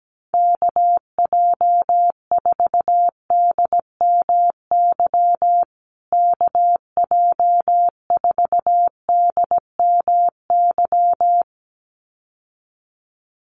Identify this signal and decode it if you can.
Morse code, heard as a keyed tone, about 17 wpm, 700 Hz. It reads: KJ4DMY KJ4DMY
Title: USA Call Signs Repeated 2x 17wpm